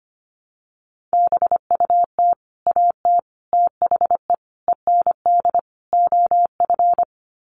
Morse code, heard as a keyed tone, about 25 wpm, 700 Hz. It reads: BUT AT THE END OF
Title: Sets of 5 Words Encore2 25wpm